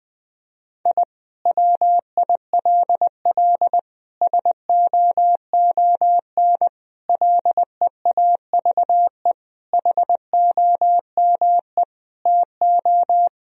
Morse code, heard as a keyed tone, about 20 wpm, 700 Hz. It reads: I WILL SOON LEAVE HOME TO
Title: Sentences from Top 1000 Words - Review Part 2 - 20wpm